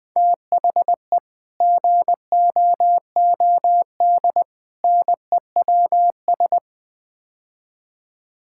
Morse code, heard as a keyed tone, about 21 wpm, 700 Hz. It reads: THE GOOD NEWS
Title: Build A Phrase - 5-Word Phrases - 20wpm